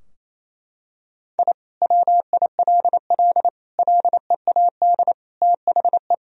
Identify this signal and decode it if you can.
Morse code, heard as a keyed tone, about 28 wpm, 700 Hz. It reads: I WILL LEAD THE
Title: Sentences from Top 1000 Words - Review Part 2 - 28wpm